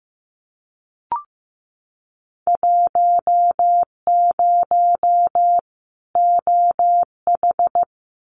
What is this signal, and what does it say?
Morse code, heard as a keyed tone, about 15 wpm, 700 Hz. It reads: 10 OH
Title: Winter Field Day Exchanges 15wpm